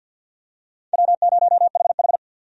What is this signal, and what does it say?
Morse code, heard as a keyed tone, about 50 wpm, 700 Hz. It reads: W0HH
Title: USA Call Signs - 4 Characters - 50wpm